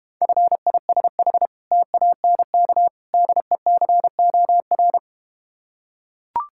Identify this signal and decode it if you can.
Morse code, heard as a keyed tone, about 32 wpm, 700 Hz. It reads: FISH TANK DECOR